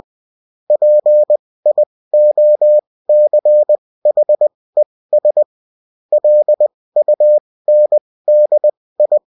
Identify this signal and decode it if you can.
Morse code, heard as a keyed tone, about 20 wpm, spaced as for 16 wpm, 600 Hz. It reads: PIOCHES LUNDI